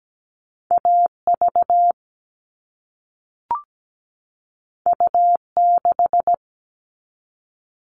Morse code, heard as a keyed tone, about 17 wpm, 700 Hz. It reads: AV U6
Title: All Cut Numbers - 2-Digit 17wpm